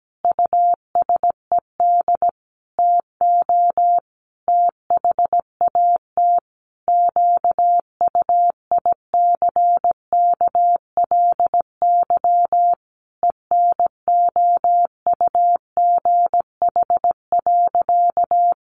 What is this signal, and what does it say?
Morse code, heard as a keyed tone, about 17 wpm, 700 Hz. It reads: USED TO THAT QUICKLY ENOUGH.
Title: Sentences from Top 700 Words 17wpm